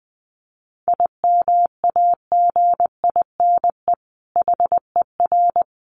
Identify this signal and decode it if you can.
Morse code, heard as a keyed tone, about 20 wpm, 700 Hz. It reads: IMAGINE HER
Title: Sentences from Top 1400 Words 20wpm